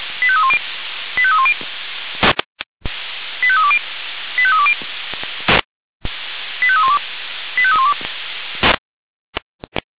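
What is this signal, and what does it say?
Five-tone selective call (ZVEI), heard as quick runs of five tones, tones of about 70 ms. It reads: 85310 85310 85320 85320 85312 85312
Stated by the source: ZVEI five-tone call received by SDR.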